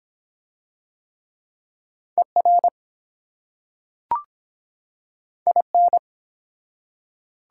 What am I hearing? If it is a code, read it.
Morse code, heard as a keyed tone, about 24 wpm, 700 Hz. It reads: ER IN